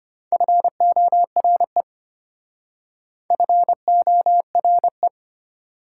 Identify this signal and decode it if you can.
Morse code, heard as a keyed tone, about 33 wpm, 700 Hz. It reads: FORE FORE